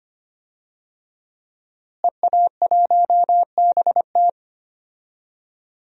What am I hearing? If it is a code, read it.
Morse code, heard as a keyed tone, about 25 wpm, 700 Hz. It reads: EA1BT
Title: DXCC-100 Call Signs - 5 Characters - Repeated 2x - 25wpm